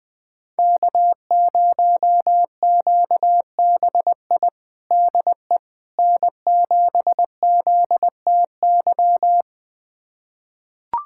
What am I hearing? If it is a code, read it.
Morse code, heard as a keyed tone, about 21 wpm, 700 Hz. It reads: K0QBI DE N7ZTY
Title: USA Callsign DE Callsign - 5 Characters - 20wpm